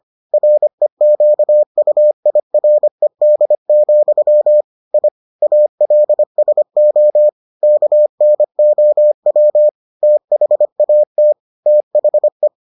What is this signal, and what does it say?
Morse code, heard as a keyed tone, about 25 wpm, 600 Hz. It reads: REQUIRED, I ALSO KNOW THAT THE